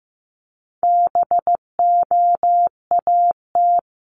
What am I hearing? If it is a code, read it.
Morse code, heard as a keyed tone, about 15 wpm, 700 Hz. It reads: BOAT
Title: Top 500 Words 15wpm